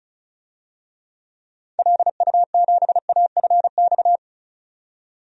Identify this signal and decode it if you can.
Morse code, heard as a keyed tone, about 34 wpm, 700 Hz. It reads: LU7AFX
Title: DXCC-100 Call Signs - 6 Characters - 35wpm